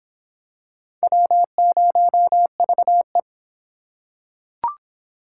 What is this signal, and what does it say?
Morse code, heard as a keyed tone, about 26 wpm, 700 Hz. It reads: W0VE